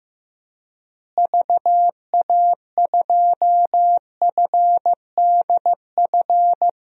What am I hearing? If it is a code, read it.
Morse code, heard as a keyed tone, about 15 wpm, 700 Hz. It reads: VA2FDF
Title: DXCC-100 Call Signs - 6 Characters - Repeated 2x - 15wpm